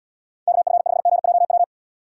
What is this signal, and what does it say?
Morse code, heard as a keyed tone, about 79 wpm, 700 Hz. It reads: KF5GJW